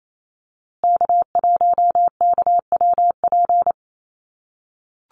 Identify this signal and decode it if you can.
Morse code, heard as a keyed tone, about 28 wpm, 700 Hz. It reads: K1KWP
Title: USA Call Signs Repeated 3x 28wpm